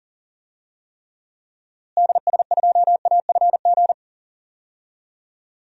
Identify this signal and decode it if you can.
Morse code, heard as a keyed tone, about 40 wpm, 700 Hz. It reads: DS2AFG
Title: DXCC-100 Call Signs - 6 Characters - Repeated 2x - 40wpm